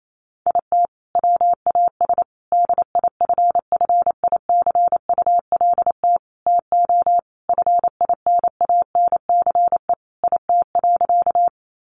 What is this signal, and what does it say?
Morse code, heard as a keyed tone, about 28 wpm, 700 Hz. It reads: IT WAS DIFFICULT TO FINANCE IT.
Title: Sentences from Top 1500 Words 28wpm